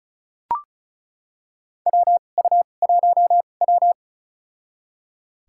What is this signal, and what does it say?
Morse code, heard as a keyed tone, about 34 wpm, spaced as for 25 wpm, 700 Hz. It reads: WU1W
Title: USA Call Signs - 4 Characters - 2x Character Spacing - 35wpm